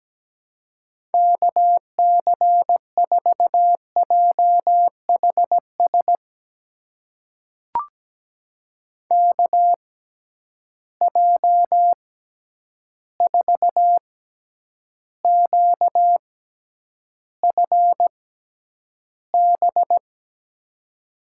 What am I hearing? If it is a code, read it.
Morse code, heard as a keyed tone, about 17 wpm, spaced as for 9 wpm, 700 Hz. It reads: KC4JHS KJ4QFB
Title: USA Call Signs - 6 Characters - 6x Character Spacing - 17wpm